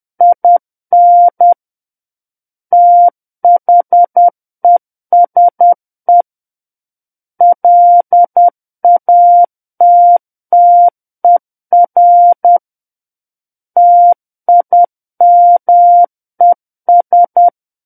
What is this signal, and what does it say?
Morse code, heard as a keyed tone, about 10 wpm, 700 Hz. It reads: IN THESE LATTER TIMES
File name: Book One / 01-Chapter 1 - 10 WPM